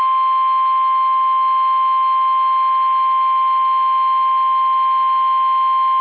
Emergency Alert System (SAME) tones (the 1050 Hz attention tone). It is heard from 0.6 to 5.8 s.
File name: eas / KUSO / KUSO